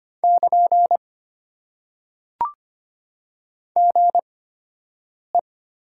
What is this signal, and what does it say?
Morse code, heard as a keyed tone, about 25 wpm, spaced as for 7 wpm, 700 Hz. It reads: <KN> GE
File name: Top 90 QSO Elements - 8x Character Spacing - 25wpm